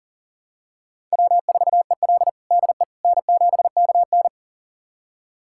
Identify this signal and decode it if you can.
Morse code, heard as a keyed tone, about 40 wpm, 700 Hz. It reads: W4EL DE N7KN
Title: USA Callsign DE Callsign - 4 Characters - 40wpm